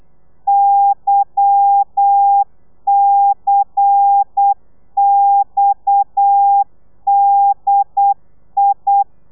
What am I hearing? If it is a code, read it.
Morse code, heard as a keyed tone, about 8 wpm, 800 Hz. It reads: YCXDI
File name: telegrafia / i2tfj / Lez10 35